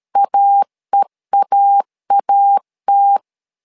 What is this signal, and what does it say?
Morse code, heard as a keyed tone, about 13 wpm, 800 Hz. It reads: AEAAT
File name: LEZIONI / 2 UHDN / 2 UH 12wpm 1439